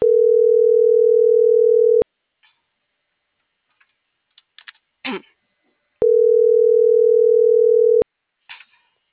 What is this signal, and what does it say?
Ringback tone (North America), 440 + 480 Hz.